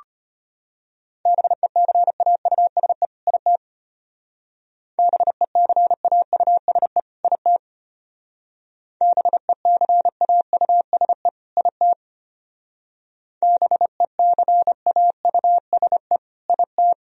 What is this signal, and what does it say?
Morse code, heard as a keyed tone, about 30 wpm, 700 Hz. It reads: BECAUSE IT BECAUSE IT BECAUSE IT BECAUSE IT
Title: Speed-Racing - Sets of 2 Words 25wpm